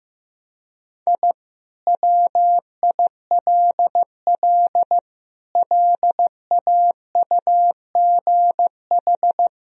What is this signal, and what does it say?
Morse code, heard as a keyed tone, about 15 wpm, 700 Hz. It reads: I WILL LAUGH
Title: Sentences from Top 1000 Words - Review Part 1 - 15wpm